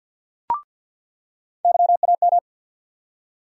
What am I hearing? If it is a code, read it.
Morse code, heard as a keyed tone, about 48 wpm, 700 Hz. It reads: YAM
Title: USA Call Sign Postfixes - 3 Letters 50wpm